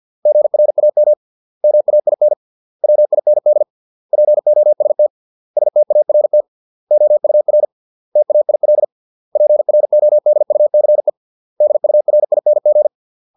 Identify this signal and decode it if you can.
Morse code, heard as a keyed tone, about 50 wpm, 600 Hz. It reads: GRAM MAIN WIND POST START OUR TAIL PRODUCE DURING